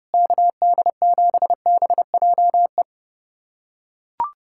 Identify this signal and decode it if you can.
Morse code, heard as a keyed tone, about 30 wpm, 700 Hz. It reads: KD7BJE
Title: USA Callsign DE Callsign 30wpm